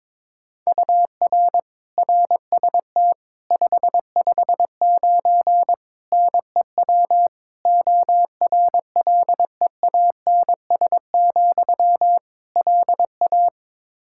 Morse code, heard as a keyed tone, about 22 wpm, 700 Hz. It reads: UR RST 559 NEW ORLEANS, LA
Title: Practice QSO - Exchange 1 - Beginning - 22wpm